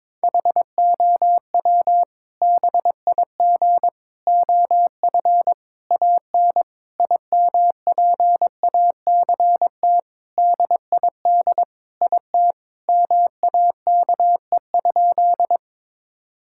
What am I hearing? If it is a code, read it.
Morse code, heard as a keyed tone, about 22 wpm, 700 Hz. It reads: HOW BIG OF AN IMPACT DID IT MAKE?
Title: Sentences from Top 1000 Words - Review Part 6 - 22wpm